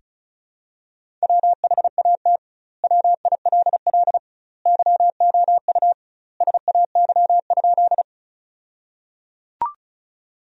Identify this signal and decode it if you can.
Morse code, heard as a keyed tone, about 36 wpm, 700 Hz. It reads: WHAT WILL YOU SAY?
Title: 2x Word-Spacing - Sentences from Top 200 Words - 35wpm